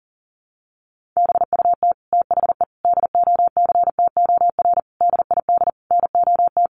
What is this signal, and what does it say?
Morse code, heard as a keyed tone, about 37 wpm, 700 Hz. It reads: BUT THE DOCTOR DID NOT
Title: Sentences from Top 1500 Words 40wpm